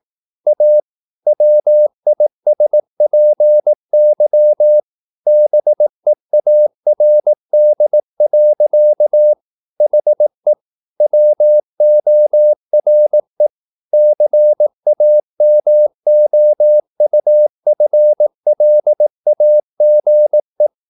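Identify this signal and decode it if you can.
Morse code, heard as a keyed tone, about 18 wpm, 600 Hz. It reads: A WISPY BEARD. HE WORE CAMOUFLAGE
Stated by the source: Morse practice at 18 wpm, English text, from CW med Gnister.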